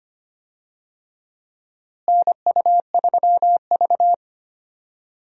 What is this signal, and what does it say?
Morse code, heard as a keyed tone, about 25 wpm, 700 Hz. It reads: NU3V
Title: USA Call Signs - 4 Characters - Repeated 3x - 25wpm